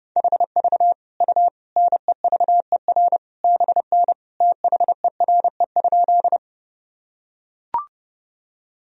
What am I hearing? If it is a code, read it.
Morse code, heard as a keyed tone, about 30 wpm, 700 Hz. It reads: HV U NEVER BN THERE?
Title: Sentences from Top 300 Words and Abbreviations 30wpm